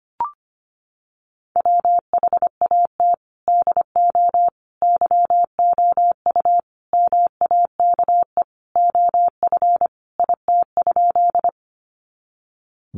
Morse code, heard as a keyed tone, about 25 wpm, 700 Hz. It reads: WHAT DO YOU MAKE OF IT?
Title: Sentences from Top 100 Words 25wpm